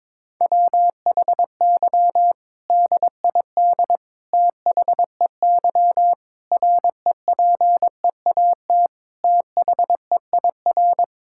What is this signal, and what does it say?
Morse code, heard as a keyed tone, about 21 wpm, 700 Hz. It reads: WHY DID THEY REPEAT THEIR
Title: Sentences from Top 1600 Words 22wpm